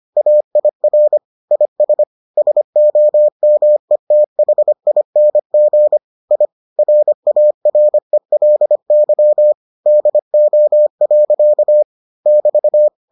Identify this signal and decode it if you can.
Morse code, heard as a keyed tone, about 25 wpm, 600 Hz. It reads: AIR IS SOMETHING I RARELY DO. <BT>